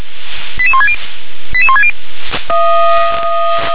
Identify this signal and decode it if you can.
Five-tone selective call (ZVEI), heard as quick runs of five tones, tones of about 70 ms. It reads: 88160 88160
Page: SDR/ZVEI